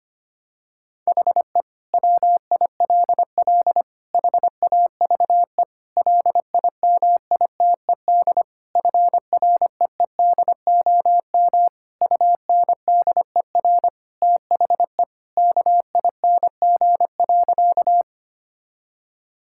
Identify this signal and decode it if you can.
Morse code, heard as a keyed tone, about 25 wpm, 700 Hz. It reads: HE WILL HAVE LIMITED FREEDOM UNDER THE KING.
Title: Sentences from Top 1000 Words - Review Part 6 - 25wpm